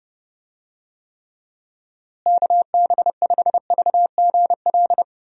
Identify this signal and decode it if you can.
Morse code, heard as a keyed tone, about 32 wpm, 700 Hz. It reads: KB5VGL
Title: USA Call Signs - 6 Characters - Repeated 3x - 30wpm